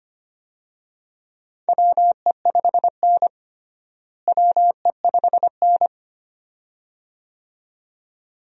Morse code, heard as a keyed tone, about 25 wpm, 700 Hz. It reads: WE5N WE5N
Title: USA Call Signs - 4 Characters - Repeated 2x - 25wpm